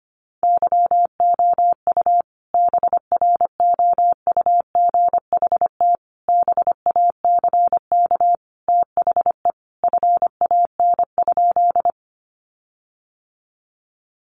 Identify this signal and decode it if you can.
Morse code, heard as a keyed tone, about 25 wpm, 700 Hz. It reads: YOU BROUGHT BACK THE FAN?
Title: Sentences from Top 1300 Words 25wpm